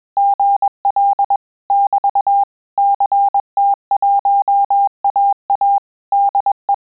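Morse code, heard as a keyed tone, about 21 wpm, 800 Hz. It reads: GL <BT> CT1AA DE
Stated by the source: Morse QSO practice at 20 words per minute, in Italian and English.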